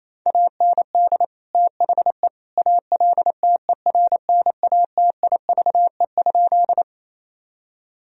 Morse code, heard as a keyed tone, about 28 wpm, 700 Hz. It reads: AND THE ALTERNATIVE?
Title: Sentences from Top 1000 Words - Review Part 4 - 28wpm